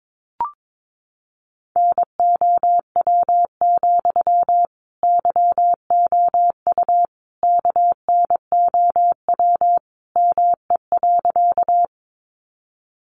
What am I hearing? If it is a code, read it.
Morse code, heard as a keyed tone, about 22 wpm, 700 Hz. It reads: NOW, YOU KNOW ME.
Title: Sentences from Top 200 Words 22wpm